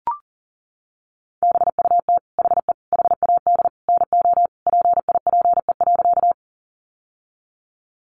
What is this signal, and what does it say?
Morse code, heard as a keyed tone, about 40 wpm, 700 Hz. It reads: BUT HE HAD NO PIPE.
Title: Sentences from Top 1500 Words 40wpm